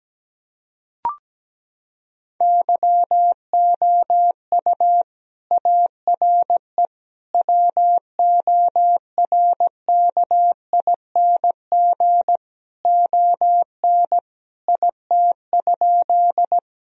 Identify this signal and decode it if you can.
Morse code, heard as a keyed tone, about 17 wpm, 700 Hz. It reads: YOU ARE WORKING ON IT?
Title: Sentences from Top 1000 Words - Review Part 4 - 17wpm